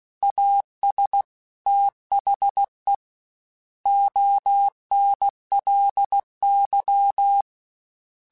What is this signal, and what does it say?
Morse code, heard as a keyed tone, about 16 wpm, 800 Hz. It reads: AS THE ONLY
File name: Technical Subjects / Technical Subjects-mp3s / AND Gate @15